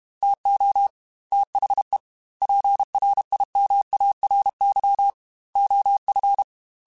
Morse code, heard as a keyed tone, about 32 wpm, 800 Hz. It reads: TO THE PRIMARY OF
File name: Technical Subjects / Technical Subjects-mp3s / Sw Pwr Sup @30